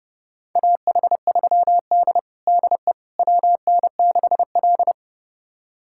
Morse code, heard as a keyed tone, about 30 wpm, 700 Hz. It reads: AH3D DE WN6L
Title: USA Callsign DE Callsign - 4 Characters - 30wpm